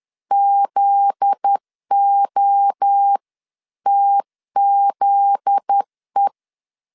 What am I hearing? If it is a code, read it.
Morse code, heard as a keyed tone, about 10 wpm, 800 Hz. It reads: ZOTZE
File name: LEZIONI / 7 GF9Z / 7 GF9Z 10wpm 2611